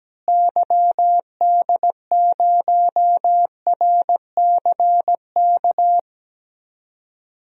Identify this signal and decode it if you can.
Morse code, heard as a keyed tone, about 17 wpm, 700 Hz. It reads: YD0RCK
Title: DXCC-100 Call Signs - 6 Characters - Repeated 3x - 17wpm